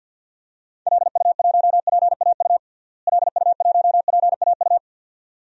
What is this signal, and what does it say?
Morse code, heard as a keyed tone, about 53 wpm, 700 Hz. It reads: LU1PAU LU1PAU
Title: DXCC-100 Call Signs - 6 Characters - Repeated 2x - 50wpm